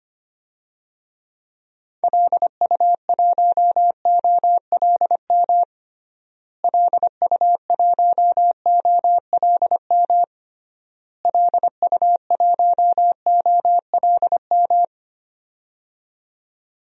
Morse code, heard as a keyed tone, about 25 wpm, 700 Hz. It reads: LU1OLM LU1OLM LU1OLM
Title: DXCC-100 Call Signs - 6 Characters - Repeated 3x - 25wpm